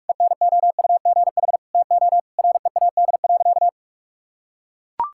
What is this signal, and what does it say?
Morse code, heard as a keyed tone, about 41 wpm, 700 Hz. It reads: ENOUGH TO READ.
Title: Sentences from Top 1000 Words - Review Part 5 - 45wpm